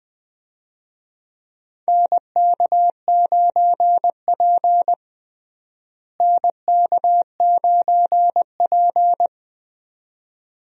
Morse code, heard as a keyed tone, about 20 wpm, 700 Hz. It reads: NK9P NK9P
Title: USA Call Signs - 4 Characters - Repeated 2x - 20wpm